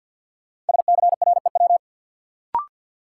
Morse code, heard as a keyed tone, about 47 wpm, 700 Hz. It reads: SCREW
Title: Sentences from Top 1600 Words - New Word Drill - 50wpm